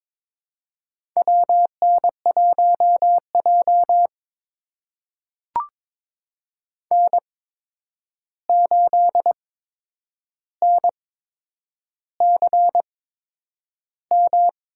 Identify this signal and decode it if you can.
Morse code, heard as a keyed tone, about 22 wpm, spaced as for 9 wpm, 700 Hz. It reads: WN1J N8NCM
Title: USA Call Signs - 8x Character Spacing - 22wpm